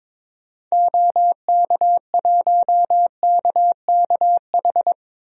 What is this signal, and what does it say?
Morse code, heard as a keyed tone, about 21 wpm, 700 Hz. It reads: OK1KKH
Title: DXCC-100 Call Signs - 6 Characters - 22wpm